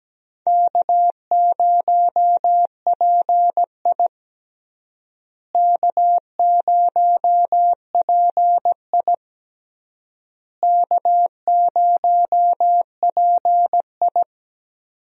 Morse code, heard as a keyed tone, about 17 wpm, 700 Hz. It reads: K0PI K0PI K0PI
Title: USA Call Signs - 4 Characters - Repeated 3x - 17wpm